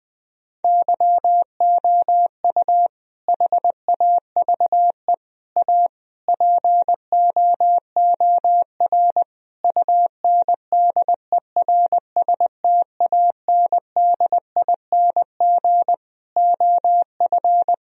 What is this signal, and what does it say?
Morse code, heard as a keyed tone, about 20 wpm, 700 Hz. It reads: YOU HAVE A POOR UNDERSTANDING OF
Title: Sentences from Top 1000 Words - Review Part 3 - 20wpm